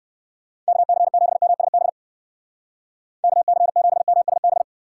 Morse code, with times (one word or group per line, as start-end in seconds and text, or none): 0.7-1.9 KC7MSB
3.2-4.6 KC7MSB